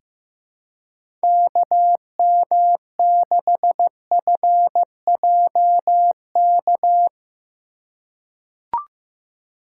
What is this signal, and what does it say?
Morse code, heard as a keyed tone, about 15 wpm, 700 Hz. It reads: KM6FJK
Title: USA Call Signs - 6 Characters - 3x Character Spacing - 15wpm